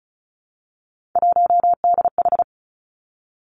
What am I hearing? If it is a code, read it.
Morse code, heard as a keyed tone, about 36 wpm, 700 Hz. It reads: 1DH
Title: Three Challenging Letter-Number Combinations 35wpm